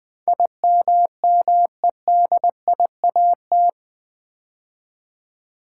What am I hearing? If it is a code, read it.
Morse code, heard as a keyed tone, about 20 wpm, 700 Hz. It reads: IMMEDIAT
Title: Build A Word - 4 Letters At A Time - 11-Letter Words - 20wpm